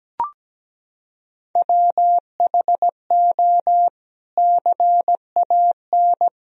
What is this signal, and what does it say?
Morse code, heard as a keyed tone, about 17 wpm, 700 Hz. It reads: WHO CAN
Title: Sentences from Top 300 Words and Abbreviations 17wpm